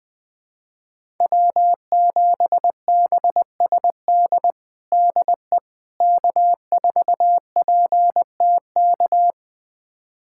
Morse code, heard as a keyed tone, about 19 wpm, 700 Hz. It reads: W7BSD DE K4PTK
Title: USA Callsign DE Callsign - 5 Characters - 20wpm